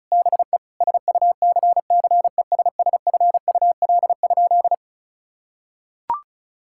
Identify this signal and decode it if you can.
Morse code, heard as a keyed tone, about 36 wpm, 700 Hz. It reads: BE SUCCESSFUL?
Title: Sentences from Top 1000 Words - Review Part 5 - 35wpm